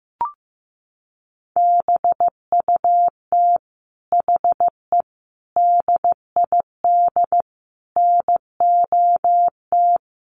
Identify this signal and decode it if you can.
Morse code, heard as a keyed tone, about 15 wpm, 700 Hz. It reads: BUT HE DID NOT
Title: Sentences from Top 1500 Words 15wpm